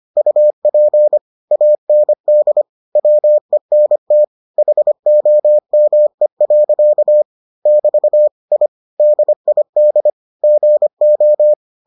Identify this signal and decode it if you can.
Morse code, heard as a keyed tone, about 25 wpm, 600 Hz. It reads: UP AND WENT HOME. <BT> I DID GO